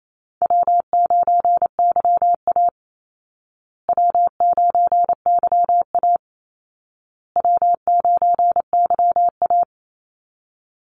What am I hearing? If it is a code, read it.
Morse code, heard as a keyed tone, about 28 wpm, 700 Hz. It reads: W9YA W9YA W9YA
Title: Easy Call Signs Repeated 3x - 28wpm